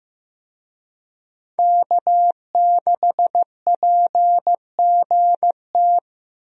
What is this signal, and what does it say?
Morse code, heard as a keyed tone, about 15 wpm, 700 Hz. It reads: K6PGT
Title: USA Call Signs - 5 Characters - Repeated 3x - 15wpm